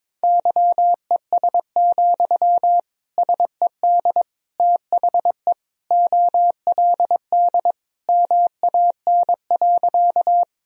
Morse code, heard as a keyed tone, about 22 wpm, 700 Hz. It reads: YES, SED THE OLD MAN.
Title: Sentences from Top 500 Words and Abbreviations 22wpm